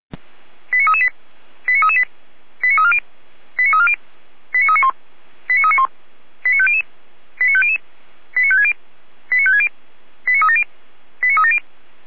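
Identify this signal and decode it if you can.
Five-tone selective call (ZVEI), heard as quick runs of five tones, tones of about 70 ms. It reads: 89338 89338 89340 89340 89391 89391 89500 89500 89560 89560 89370 89370
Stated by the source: Received on SDR; station feuerwehr-orlamuende.org (fire brigade Orlamünde).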